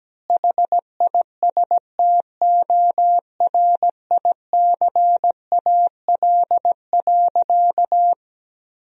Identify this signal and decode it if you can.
Morse code, heard as a keyed tone, about 17 wpm, 700 Hz. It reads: HISTORICAL.
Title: Sentences from Top 1000 Words - Review Part 4 - 17wpm